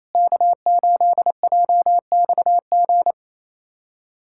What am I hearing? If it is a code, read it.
Morse code, heard as a keyed tone, about 28 wpm, 700 Hz. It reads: K8JXG
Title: USA Callsign DE Callsign 28wpm